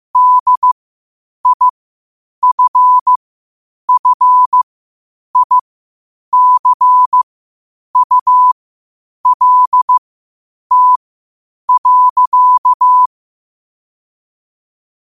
Morse code, heard as a keyed tone, about 15 wpm, spaced as for 8 wpm, 1000 Hz. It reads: DIFFICULT.
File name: qotdfiles / 8wpm